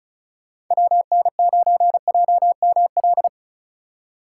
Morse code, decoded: WN9JML

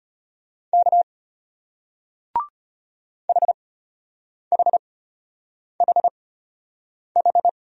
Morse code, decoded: K HHHH

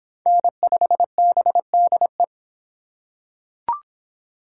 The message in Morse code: N5BDE

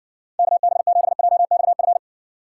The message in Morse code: KB7JXF

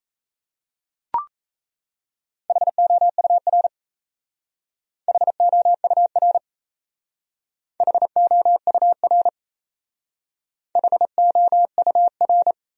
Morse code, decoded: HOUR HOUR HOUR HOUR

